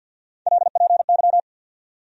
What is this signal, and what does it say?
Morse code, heard as a keyed tone, about 51 wpm, 700 Hz. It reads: LPY
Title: USA Call Sign Postfixes - 3 Letters 50wpm